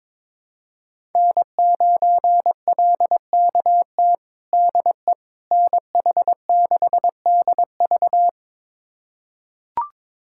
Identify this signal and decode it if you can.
Morse code, heard as a keyed tone, about 21 wpm, 700 Hz. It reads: N9LKT DE NH6DV
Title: USA Callsign DE Callsign - 5 Characters - 22wpm